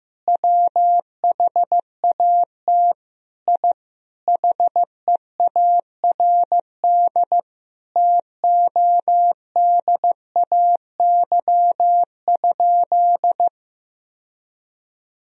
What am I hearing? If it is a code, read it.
Morse code, heard as a keyed tone, about 15 wpm, 700 Hz. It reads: WHAT I HEARD TODAY?
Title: Sentences from Top 1000 Words - Review Part 4 - 15wpm